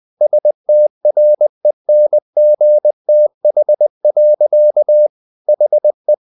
Morse code, heard as a keyed tone, about 20 wpm, 600 Hz. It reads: STRENGTH. HE
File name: the old man and the sea part 2 20wpm